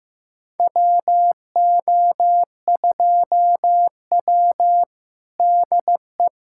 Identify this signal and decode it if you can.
Morse code, heard as a keyed tone, about 15 wpm, 700 Hz. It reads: WO2W DE
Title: USA Callsign DE Callsign - 4 Characters - 15wpm